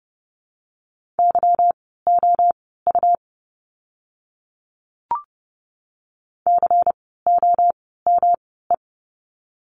Morse code, heard as a keyed tone, about 30 wpm, spaced as for 17 wpm, 700 Hz. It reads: YOU COME
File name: course / 30:10wpm / 237 Lesson 38f - Letter q - Words 30wpm